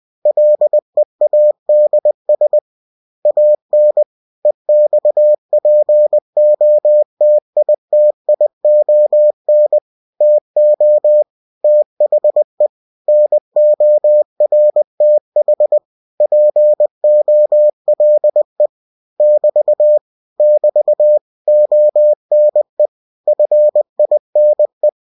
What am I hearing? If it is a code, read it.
Morse code, heard as a keyed tone, about 20 wpm, 600 Hz. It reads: LEADS AN EXPOTITION TO THE NORTH POLE <BT> <BT> ONE FINE